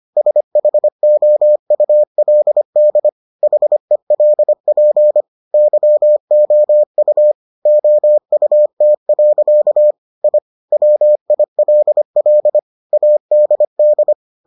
Morse code, heard as a keyed tone, about 25 wpm, 600 Hz. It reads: SHOULD HELP YOU OUT. I WILL ADD